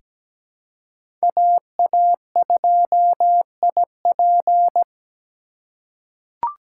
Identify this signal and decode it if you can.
Morse code, heard as a keyed tone, about 17 wpm, 700 Hz. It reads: AA2IP
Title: USA Call Signs - 5 Characters - 3x Character Spacing - 17wpm